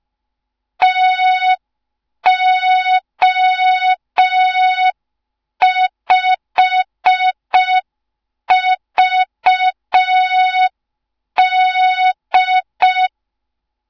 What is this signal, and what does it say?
Morse code, heard as a keyed tone, about 5 wpm, 3700 Hz. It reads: TO5VD
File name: Morze-HA2MN / hang / 07 GYAKORLAT